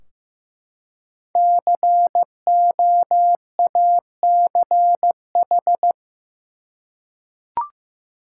Morse code, heard as a keyed tone, about 15 wpm, 700 Hz. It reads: COACH
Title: Five Letter Words - 15wpm